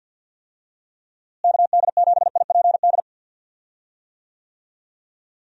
Morse code, heard as a keyed tone, about 51 wpm, 700 Hz. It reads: KD7IPD
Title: USA Call Signs - 6 Characters - Repeated 2x - 50wpm